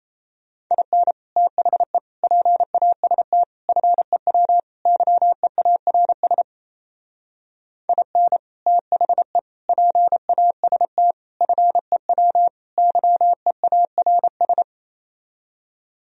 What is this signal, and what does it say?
Morse code, heard as a keyed tone, about 36 wpm, 700 Hz. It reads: IN THE PAST FEW YEARS IN THE PAST FEW YEARS